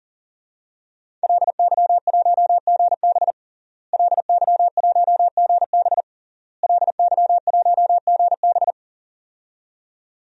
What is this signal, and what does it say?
Morse code, heard as a keyed tone, about 40 wpm, 700 Hz. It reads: LY1GB LY1GB LY1GB
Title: DXCC-100 Call Signs - 5 Characters - Repeated 3x - 40wpm